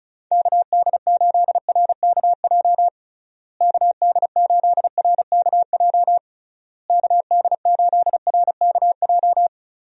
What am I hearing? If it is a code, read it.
Morse code, heard as a keyed tone, about 34 wpm, 700 Hz. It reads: KD8RKJ KD8RKJ KD8RKJ
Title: USA Call Signs - 6 Characters - Repeated 3x - 35wpm